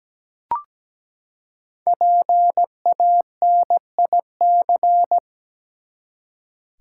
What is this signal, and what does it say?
Morse code, heard as a keyed tone, about 17 wpm, 700 Hz. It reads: PANIC
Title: Five Letter Words - 17wpm